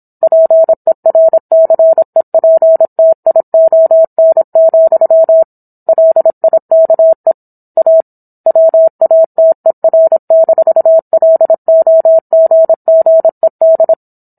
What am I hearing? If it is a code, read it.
Morse code, heard as a keyed tone, about 27 wpm, 650 Hz. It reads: PERCEPTION, LIKE A WATER-LOGGED